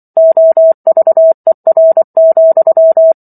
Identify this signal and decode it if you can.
Morse code, heard as a keyed tone, about 24 wpm, 650 Hz. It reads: OVER,